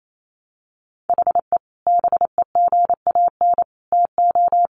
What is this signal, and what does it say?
Morse code, heard as a keyed tone, about 28 wpm, 700 Hz. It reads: HE BEGAN TO
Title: Sentences from Top 1100 Words 28wpm